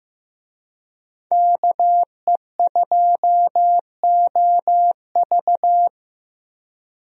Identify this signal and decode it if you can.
Morse code, heard as a keyed tone, about 15 wpm, 700 Hz. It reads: KE2OV
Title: USA Call Signs - 5 Characters - Repeated 3x - 15wpm